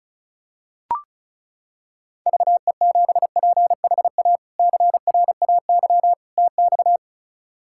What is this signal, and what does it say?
Morse code, heard as a keyed tone, about 34 wpm, 700 Hz. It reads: VE7PHA CRAY TX
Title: K1USN SST QSO Practice - 35wpm